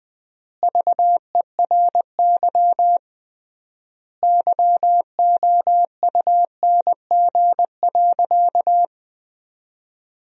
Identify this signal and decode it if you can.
Morse code, heard as a keyed tone, about 20 wpm, 700 Hz. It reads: VERY YOUNG.
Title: 3x Word-Spacing - Sentences from Top 400 Words - 20wpm